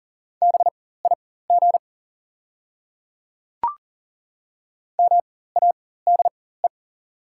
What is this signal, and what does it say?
Morse code, heard as a keyed tone, about 40 wpm, spaced as for 19 wpm, 690 Hz. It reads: BIG MADE
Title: 4x Character Spacing - Top 300 Words - 40wpm